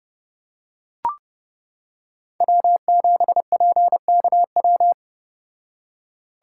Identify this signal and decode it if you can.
Morse code, heard as a keyed tone, about 29 wpm, 700 Hz. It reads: W7PKW